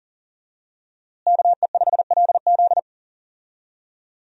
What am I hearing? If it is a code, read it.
Morse code, heard as a keyed tone, about 40 wpm, 700 Hz. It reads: KE5LZ
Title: USA Call Signs - 5 Characters - Repeated 3x - 40wpm